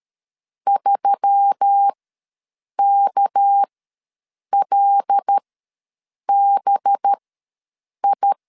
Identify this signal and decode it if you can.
Morse code, heard as a keyed tone, about 13 wpm, spaced as for 7 wpm, 800 Hz. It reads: 3KLBI
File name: LEZIONI / 6 PKX0 / 6 PKX0 ws 12wpm 3448